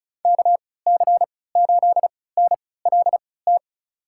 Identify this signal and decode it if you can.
Morse code, heard as a keyed tone, about 34 wpm, spaced as for 20 wpm, 700 Hz. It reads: KC8NLT
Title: USA Call Signs - 6 Characters - 3x Character Spacing - 35wpm